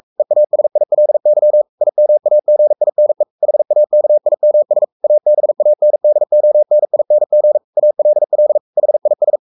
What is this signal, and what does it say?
Morse code, heard as a keyed tone, about 44 wpm, 600 Hz. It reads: EASILY IMAGINE HAKIMS ABANDONING ALL HIS
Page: Romanen Never af Ken Follet på engelsk.